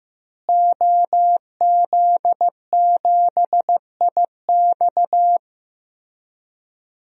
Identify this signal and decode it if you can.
Morse code, heard as a keyed tone, about 15 wpm, 700 Hz. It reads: OZ7IX
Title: DXCC-100 Call Signs - 5 Characters - Repeated 3x - 15wpm